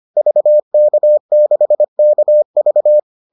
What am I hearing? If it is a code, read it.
Morse code, heard as a keyed tone, about 25 wpm, 600 Hz. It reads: VK6KV